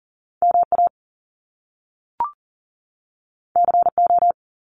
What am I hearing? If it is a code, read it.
Morse code, heard as a keyed tone, about 42 wpm, 700 Hz. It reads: MA CO
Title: US States 40wpm